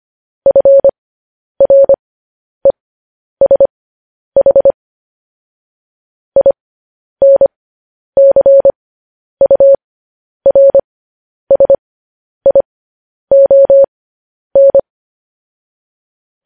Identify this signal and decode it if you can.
Morse code, heard as a keyed tone, about 25 wpm, spaced as for 10 wpm, 550 Hz. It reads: FRESH INCURSION
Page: News Headlines in Morse Code at 10 WPM